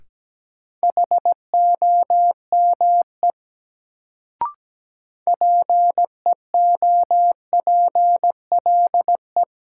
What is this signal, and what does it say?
Morse code, heard as a keyed tone, about 17 wpm, 700 Hz. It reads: HOME PEOPLE
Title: Sentences from Top 200 Words - New Word Drill - 17wpm